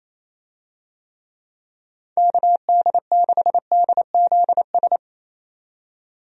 Morse code, heard as a keyed tone, about 27 wpm, 700 Hz. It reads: KD6DZS
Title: USA Call Signs - 6 Characters - 28wpm